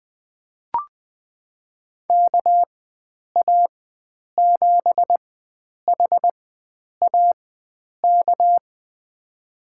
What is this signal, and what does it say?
Morse code, heard as a keyed tone, about 20 wpm, spaced as for 9 wpm, 700 Hz. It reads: KA7HAK
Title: USA Call Signs - 6 Characters - 4x Character Spacing - 20wpm